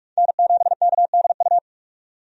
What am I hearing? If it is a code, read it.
Morse code, heard as a keyed tone, about 47 wpm, 700 Hz. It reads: N7KDU
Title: USA Call Signs - 5 Characters - 45wpm